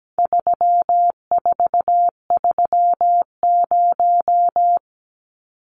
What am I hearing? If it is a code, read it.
Morse code, heard as a keyed tone, about 17 wpm, 700 Hz. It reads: 3430
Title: Numbers - 4-Digit 17wpm